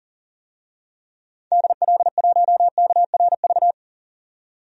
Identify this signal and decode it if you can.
Morse code, heard as a keyed tone, about 40 wpm, 700 Hz. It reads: DL1KRV
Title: DXCC-100 Call Signs - 6 Characters - 40wpm